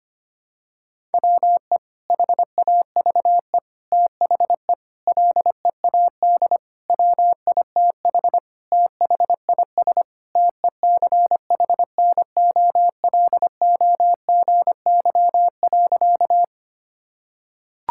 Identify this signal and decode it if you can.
Morse code, heard as a keyed tone, about 25 wpm, 700 Hz. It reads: WE HAVE THE LEAD WITH THIS TECHNOLOGY.
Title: Sentences from Top 1000 Words - Review Part 5 - 25wpm